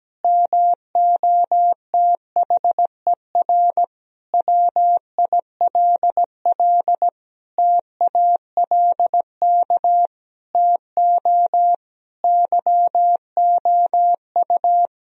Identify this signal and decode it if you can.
Morse code, heard as a keyed tone, about 17 wpm, 700 Hz. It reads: MOTHER WILL TALK TO YOU
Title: Sentences from Top 1000 Words - Review Part 1 - 17wpm